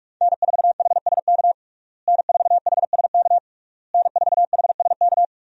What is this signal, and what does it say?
Morse code, heard as a keyed tone, about 46 wpm, 700 Hz. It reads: N4HSK N4HSK N4HSK
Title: USA Call Signs - 5 Characters - Repeated 3x - 45wpm